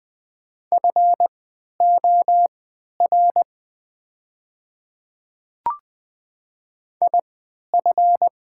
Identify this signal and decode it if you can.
Morse code, heard as a keyed tone, about 20 wpm, spaced as for 11 wpm, 690 Hz. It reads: FOR IF
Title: 3x Character Spacing - Top 100 Words - 20wpm